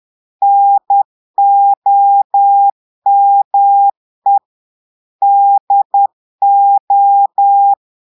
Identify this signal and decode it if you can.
Morse code, heard as a keyed tone, about 10 wpm, 800 Hz. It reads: NOME DO
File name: portugais texte en clair 2 vitesse 10 mots minutes